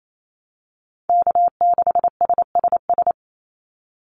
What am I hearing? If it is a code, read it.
Morse code, heard as a keyed tone, about 28 wpm, 700 Hz. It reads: K6SSS
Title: Call Signs Repeated 3x - 28wpm